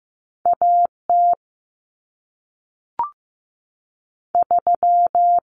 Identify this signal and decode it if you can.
Morse code, heard as a keyed tone, about 15 wpm, 700 Hz. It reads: AT 3
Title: Common Cut Numbers - 2-Digit 15wpm